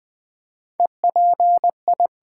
Morse code, heard as a keyed tone, about 20 wpm, 700 Hz. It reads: EPI
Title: Build A Word - 4 Letters At A Time - 8-Letter Words - 20wpm